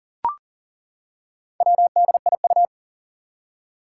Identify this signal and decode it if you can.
Morse code, heard as a keyed tone, about 40 wpm, 700 Hz. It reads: WDIU
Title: Four Challenging Letter-Number Combinations 40wpm